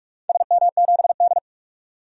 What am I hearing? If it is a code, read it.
Morse code, heard as a keyed tone, about 47 wpm, 700 Hz. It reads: SM7D
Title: DXCC-100 Call Signs - 4 Characters - 45wpm